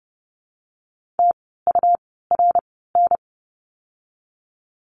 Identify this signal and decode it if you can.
Morse code, heard as a keyed tone, about 30 wpm, spaced as for 17 wpm, 700 Hz. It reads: TURN
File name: course / 30:10wpm / 237 Lesson 38f - Letter q - Words 30wpm